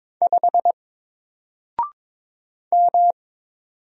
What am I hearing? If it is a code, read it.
Morse code, heard as a keyed tone, about 22 wpm, 700 Hz. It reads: 5 M
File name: Single-Letter-Number-22wpm